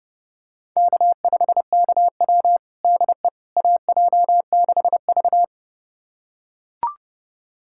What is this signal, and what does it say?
Morse code, heard as a keyed tone, about 32 wpm, 700 Hz. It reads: K5KW DE AJ6V
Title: USA Callsign DE Callsign - 4 Characters - 30wpm